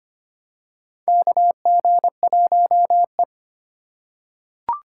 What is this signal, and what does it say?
Morse code, heard as a keyed tone, about 25 wpm, 700 Hz. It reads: KG1E
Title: USA Call Signs - 4 Characters - 2x Character Spacing - 25wpm